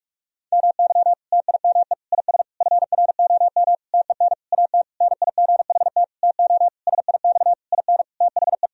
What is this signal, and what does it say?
Morse code, heard as a keyed tone, about 46 wpm, 700 Hz. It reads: MY TIME IS FROM TEN AT NIGHT TO SIX IN THE
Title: Sentences from Top 1000 Words - Review Part 2 - 45wpm